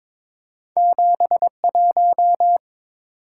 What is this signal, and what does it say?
Morse code, heard as a keyed tone, about 30 wpm, spaced as for 26 wpm, 700 Hz. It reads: 71
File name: Speed-Racing - Two Letter-Number Combinations - 22wpm